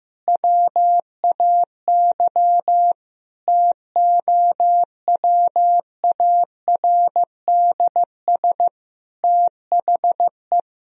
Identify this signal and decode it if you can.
Morse code, heard as a keyed tone, about 15 wpm, 700 Hz. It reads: WAY TOWARDS THE
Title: Sentences from Top 1000 Words - Review Part 2 - 15wpm